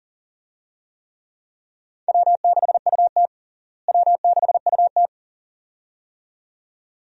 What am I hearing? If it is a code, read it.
Morse code, heard as a keyed tone, about 40 wpm, 700 Hz. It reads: W6UT W6UT
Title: USA Call Signs - 4 Characters - Repeated 2x - 40wpm